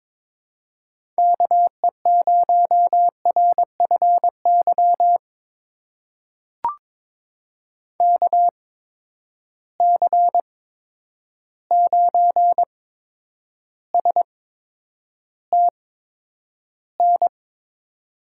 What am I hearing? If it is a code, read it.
Morse code, heard as a keyed tone, about 22 wpm, spaced as for 9 wpm, 700 Hz. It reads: KE0RFY KC9STN
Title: USA Call Signs - 6 Characters - 8x Character Spacing - 22wpm